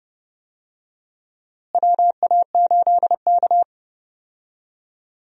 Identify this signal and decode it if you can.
Morse code, heard as a keyed tone, about 30 wpm, 700 Hz. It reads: WA8K
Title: USA Call Signs - 4 Characters - Repeated 2x - 30wpm